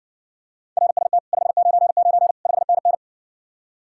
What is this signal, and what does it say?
Morse code, heard as a keyed tone, about 60 wpm, 700 Hz. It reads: RST 599 5NN